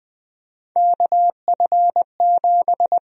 Morse code, heard as a keyed tone, about 20 wpm, 700 Hz. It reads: KF7
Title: USA Call Sign Prefixes - Repeated 2x - 20wpm